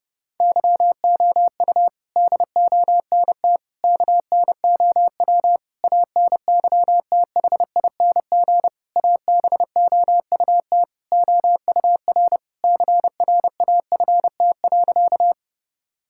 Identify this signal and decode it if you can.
Morse code, heard as a keyed tone, about 30 wpm, 700 Hz. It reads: YOU DONT KNOW ANYTHING ABOUT OUR CRAFT.
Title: Sentences from Top 1000 Words - Review Part 6 - 30wpm